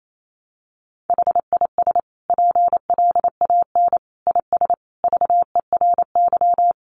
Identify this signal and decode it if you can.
Morse code, heard as a keyed tone, about 28 wpm, 700 Hz. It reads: HIS PLAN IS VERY
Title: Sentences from Top 500 Words 28wpm